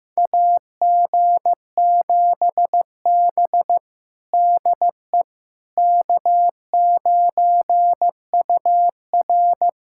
Morse code, heard as a keyed tone, about 15 wpm, 700 Hz. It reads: AG7B DE K9UR
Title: USA Callsign DE Callsign - 4 Characters - 15wpm